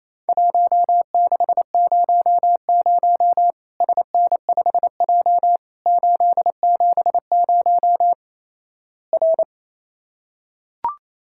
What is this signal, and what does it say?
Morse code, heard as a keyed tone, about 28 wpm, 700 Hz. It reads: 1600 SN5J 870 R
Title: Worked All Europe DX Contest - QTC Exchange - 28wpm